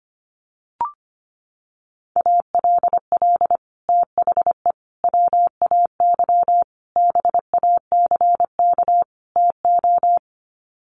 Morse code, heard as a keyed tone, about 25 wpm, 700 Hz. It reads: ALL THE WAY BACK TO
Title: Sets of 5 Words Encore 25wpm